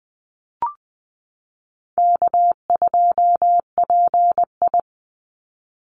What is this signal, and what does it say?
Morse code, heard as a keyed tone, about 20 wpm, 700 Hz. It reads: K2PI
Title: Easy Call Signs Repeated 2x - 20wpm